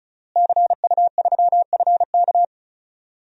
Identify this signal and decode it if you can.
Morse code, heard as a keyed tone, about 36 wpm, 700 Hz. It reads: CU3FK
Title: DXCC-100 Call Signs - 5 Characters - Repeated 3x - 35wpm